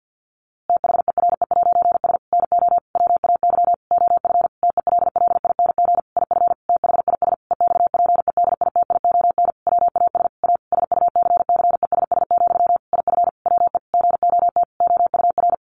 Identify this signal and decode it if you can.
Morse code, heard as a keyed tone, about 50 wpm, 700 Hz. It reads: THERE'S NO WAY OF TELLING IF THIS EXPEDITION WAS A SUCCESS, IF WE GOT OUR
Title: A Sound Of Thunder 50wpm